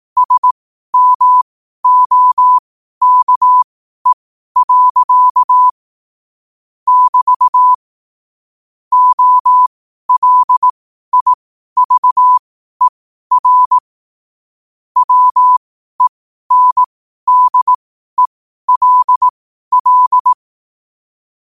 Morse code, heard as a keyed tone, about 18 wpm, spaced as for 12 wpm, 1000 Hz. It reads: SMOKE. <BT> OLIVER WENDELL
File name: qotdfiles / 12wpm